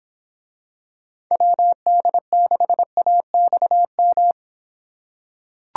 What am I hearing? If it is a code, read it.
Morse code, heard as a keyed tone, about 26 wpm, 700 Hz. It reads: WD6AXM